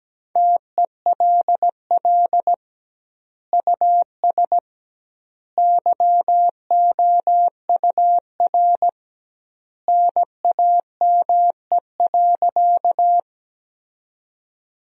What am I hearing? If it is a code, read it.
Morse code, heard as a keyed tone, about 17 wpm, 700 Hz. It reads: TELL US YOUR NAME.
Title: 2x Word-Spacing - Sentences from Top 200 Words - 17wpm